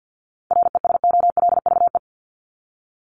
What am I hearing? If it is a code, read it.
Morse code, heard as a keyed tone, about 50 wpm, 700 Hz. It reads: RESOLVE
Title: Top 1751-2000 Words 50wpm